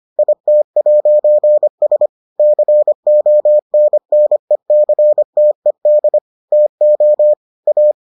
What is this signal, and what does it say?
Morse code, heard as a keyed tone, about 25 wpm, 600 Hz. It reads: IT'S CONNECTED TO A